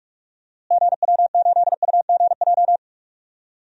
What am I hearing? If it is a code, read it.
Morse code, heard as a keyed tone, about 44 wpm, 700 Hz. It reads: GW8UGJ
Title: DXCC-100 Call Signs - 6 Characters - Repeated 3x - 45wpm